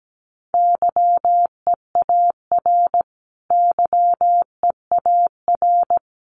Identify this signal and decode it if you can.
Morse code, heard as a keyed tone, about 17 wpm, 700 Hz. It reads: YEAR YEAR
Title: Top 500 Words Repeated 17wpm